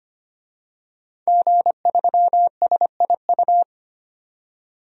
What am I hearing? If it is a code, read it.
Morse code, heard as a keyed tone, about 25 wpm, 700 Hz. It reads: G3SIU
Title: DXCC-100 Call Signs - 5 Characters - 25wpm